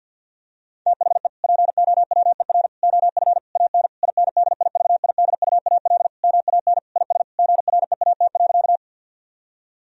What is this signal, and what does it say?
Morse code, heard as a keyed tone, about 50 wpm, 700 Hz. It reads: THE POWER OF AN INDIVIDUAL MAN IS GREAT.
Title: Sentences from Top 1000 Words - Review Part 4 - 50wpm